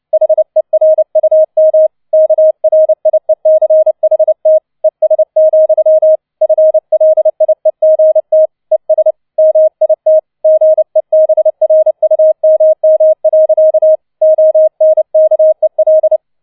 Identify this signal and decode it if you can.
Morse code, heard as a keyed tone, about 29 wpm, 630 Hz. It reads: HERUM KRIECHT ES, FLIEGT ES MIT GEBRUMM. ONKEL